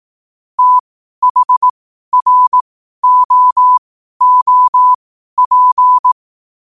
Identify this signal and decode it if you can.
Morse code, heard as a keyed tone, about 18 wpm, spaced as for 12 wpm, 1000 Hz. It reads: THROOP
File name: qotdfiles / 12wpm